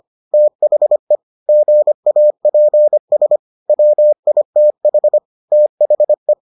Morse code, heard as a keyed tone, about 25 wpm, 600 Hz. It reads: THE GAPS WITH THE